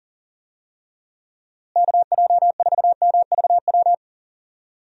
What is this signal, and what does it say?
Morse code, heard as a keyed tone, about 40 wpm, 700 Hz. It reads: KJ4MVW
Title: USA Call Signs - 6 Characters - Repeated 3x - 40wpm